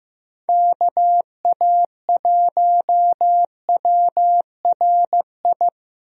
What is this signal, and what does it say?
Morse code, heard as a keyed tone, about 15 wpm, 700 Hz. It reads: KA1WRI
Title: USA Call Signs - 6 Characters - Repeated 2x - 15wpm